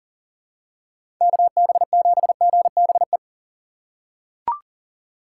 Morse code, heard as a keyed tone, about 40 wpm, 700 Hz. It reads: KB7GBE
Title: USA Call Signs - 6 Characters - 4x Character Spacing - 40wpm